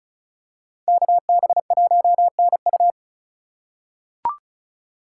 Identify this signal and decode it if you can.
Morse code, heard as a keyed tone, about 36 wpm, 700 Hz. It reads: KB1NU
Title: USA Call Signs - 5 Characters - 4x Character Spacing - 35wpm